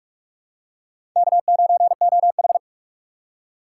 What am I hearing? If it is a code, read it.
Morse code, heard as a keyed tone, about 43 wpm, 700 Hz. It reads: K9OH
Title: USA Call Signs - 2x Character Spacing - 45wpm